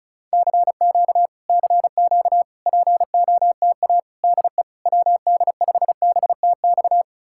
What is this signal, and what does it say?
Morse code, heard as a keyed tone, about 40 wpm, 700 Hz. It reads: CQ CQ POTA DE WD5BTX